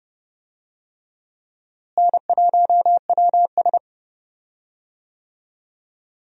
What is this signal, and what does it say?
Morse code, heard as a keyed tone, about 30 wpm, 700 Hz. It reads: N1WS
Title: USA Call Signs - 4 Characters - Repeated 3x - 30wpm